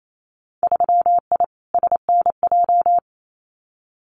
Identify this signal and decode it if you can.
Morse code, heard as a keyed tone, about 28 wpm, 700 Hz. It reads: 3I SNJ
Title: Winter Field Day Exchanges Encore 28wpm